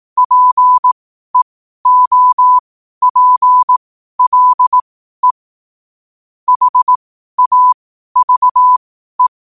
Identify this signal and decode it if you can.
Morse code, heard as a keyed tone, about 18 wpm, spaced as for 12 wpm, 1000 Hz. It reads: PEOPLE HAVE
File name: qotdfiles / 12wpm